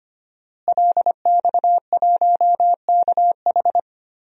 Morse code, heard as a keyed tone, about 25 wpm, 700 Hz. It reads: LX1KH
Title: DXCC-100 Call Signs - 5 Characters - Repeated 3x - 25wpm